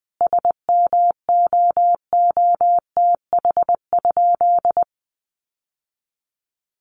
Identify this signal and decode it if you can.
Morse code, heard as a keyed tone, about 20 wpm, 700 Hz. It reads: SMOOTH?
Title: Sentences from Top 1400 Words 20wpm